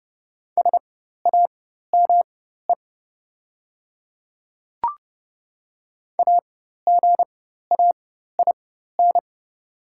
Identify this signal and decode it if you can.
Morse code, heard as a keyed tone, about 30 wpm, spaced as for 14 wpm, 700 Hz. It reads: SAME AGAIN
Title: 4x Character Spacing - Top 200 Words - 30wpm